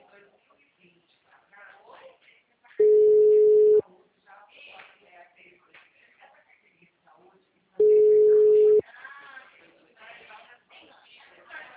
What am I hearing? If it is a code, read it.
Ringback tone (Europe, 425 Hz), with speech.